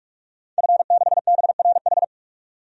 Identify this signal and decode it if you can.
Morse code, heard as a keyed tone, about 46 wpm, 700 Hz. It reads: F6BRH
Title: DXCC-100 Call Signs - 5 Characters - 45wpm